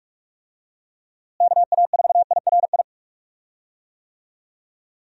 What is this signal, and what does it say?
Morse code, heard as a keyed tone, about 45 wpm, 700 Hz. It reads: KA4IRI
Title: USA Call Signs - 6 Characters - Repeated 2x - 45wpm